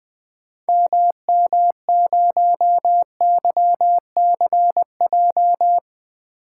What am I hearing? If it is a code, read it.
Morse code, heard as a keyed tone, about 20 wpm, 700 Hz. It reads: MM0YCJ
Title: 2023 SOTA Activator Call Signs - 20wpm